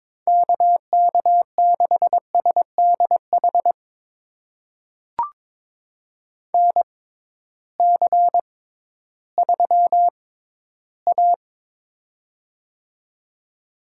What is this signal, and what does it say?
Morse code, heard as a keyed tone, about 22 wpm, spaced as for 13 wpm, 700 Hz. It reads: KK6SDH NC3A
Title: USA Call Signs - 6x Character Spacing - 22wpm